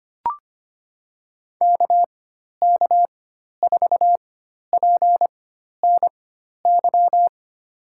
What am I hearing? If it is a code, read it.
Morse code, heard as a keyed tone, about 25 wpm, spaced as for 12 wpm, 700 Hz. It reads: KK4PNY
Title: USA Call Signs - 4x Character Spacing - 25wpm